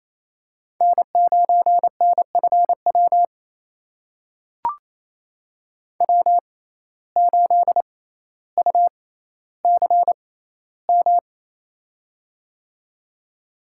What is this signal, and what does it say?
Morse code, heard as a keyed tone, about 28 wpm, spaced as for 14 wpm, 700 Hz. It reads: N9NFW W8UCM
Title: USA Call Signs - 6x Character Spacing - 28wpm